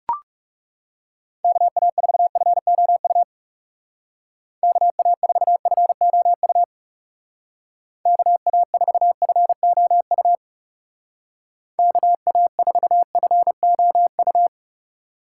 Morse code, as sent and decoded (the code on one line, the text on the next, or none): -.- .- ....- ..-. --- ..- / -.- .- ....- ..-. --- ..- / -.- .- ....- ..-. --- ..- / -.- .- ....- ..-. --- ..-
KA4FOU KA4FOU KA4FOU KA4FOU